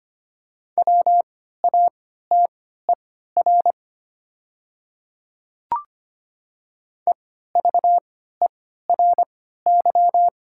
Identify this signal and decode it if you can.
Morse code, heard as a keyed tone, about 25 wpm, spaced as for 14 wpm, 700 Hz. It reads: WATER EVERY
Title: 3x Character Spacing - Top 500 Words - 25wpm